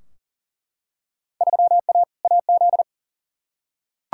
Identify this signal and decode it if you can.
Morse code, heard as a keyed tone, about 40 wpm, 700 Hz. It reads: 3A AZ
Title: Field Day Exchanges Encore - 40wpm